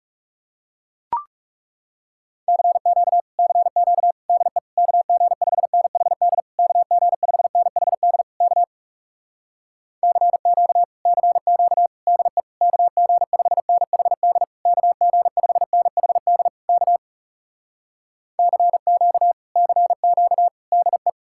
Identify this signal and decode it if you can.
Morse code, heard as a keyed tone, about 40 wpm, 700 Hz. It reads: CQ CQ DE KG5NHD KG5NHD K CQ CQ DE KG5NHD KG5NHD K CQ CQ DE